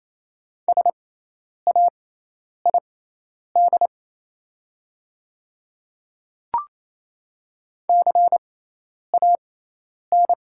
Morse code, heard as a keyed tone, about 28 wpm, spaced as for 10 wpm, 700 Hz. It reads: SAID CAN
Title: 6x Character Spacing - Top 100 Words - 28wpm